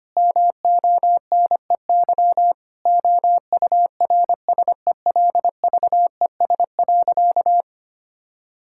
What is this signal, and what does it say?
Morse code, heard as a keyed tone, about 25 wpm, 700 Hz. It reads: MONEY OURSELVES.
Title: Sentences from Top 1600 Words 25wpm